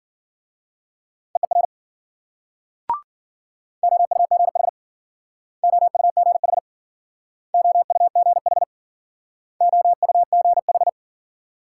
Morse code, heard as a keyed tone, about 53 wpm, 700 Hz. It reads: EER OUGH OUGH OUGH OUGH